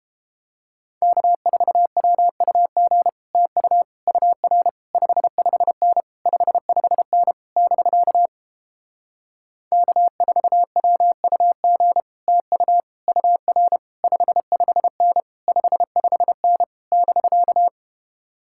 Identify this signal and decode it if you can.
Morse code, heard as a keyed tone, about 29 wpm, 700 Hz. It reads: K4WUG TU UR 55N 55N <BK> K4WUG TU UR 55N 55N <BK>